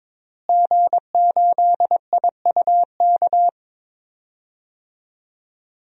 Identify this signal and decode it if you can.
Morse code, heard as a keyed tone, about 22 wpm, 700 Hz. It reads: G8IUK
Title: DXCC-100 Call Signs - 5 Characters - Repeated 3x - 22wpm